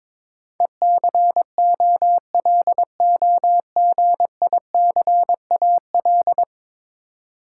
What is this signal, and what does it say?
Morse code, heard as a keyed tone, about 22 wpm, 700 Hz. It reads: ECOLOGICAL
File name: practice / 22wpm / 10 Letter Words - First 3 Letters Given - 22wpm